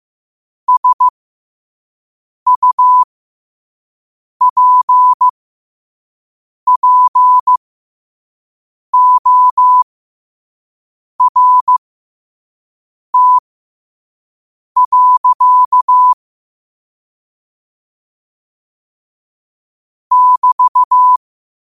Morse code, heard as a keyed tone, about 15 wpm, spaced as for 5 wpm, 1000 Hz. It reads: SUPPORT. <BT>
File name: qotdfiles / 5wpm